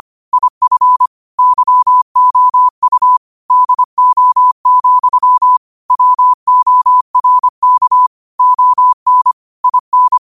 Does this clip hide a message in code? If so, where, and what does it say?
Yes: Morse code from 0.3 to 10.2 s, heard as a keyed tone, about 25 wpm, 1000 Hz. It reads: IF YOU DO, WORK ON IN